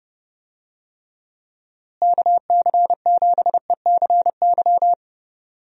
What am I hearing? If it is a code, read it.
Morse code, heard as a keyed tone, about 30 wpm, 700 Hz. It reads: KC7ECY
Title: USA Call Signs - 6 Characters - Repeated 3x - 30wpm